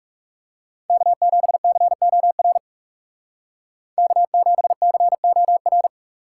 Morse code, heard as a keyed tone, about 39 wpm, 700 Hz. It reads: K7COR K7COR